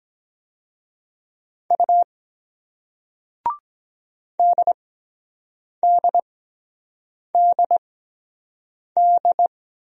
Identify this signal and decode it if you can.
Morse code, heard as a keyed tone, about 23 wpm, spaced as for 7 wpm, 700 Hz. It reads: U DDDD